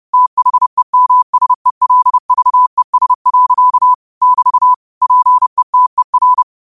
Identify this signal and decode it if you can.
Morse code, heard as a keyed tone, about 30 wpm, 1000 Hz. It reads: THEMSELVES. <BT> PETER
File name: 30wpm